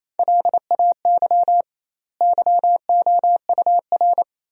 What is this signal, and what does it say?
Morse code, heard as a keyed tone, about 28 wpm, 700 Hz. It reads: LAY YOUR
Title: 2x Word-Spacing - Sentences from Top 400 Words - 28wpm